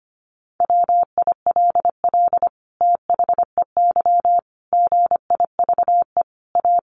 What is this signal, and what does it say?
Morse code, heard as a keyed tone, about 25 wpm, 700 Hz. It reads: WILL THEY GIVE A
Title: Sentences from Top 1400 Words 25wpm